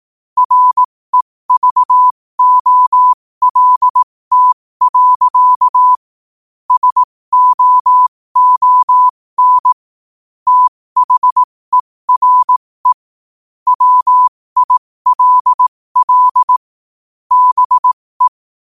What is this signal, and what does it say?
Morse code, heard as a keyed tone, about 18 wpm, spaced as for 15 wpm, 1000 Hz. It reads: REVOLT. SOON THERE WILL BE
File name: qotdfiles / 15wpm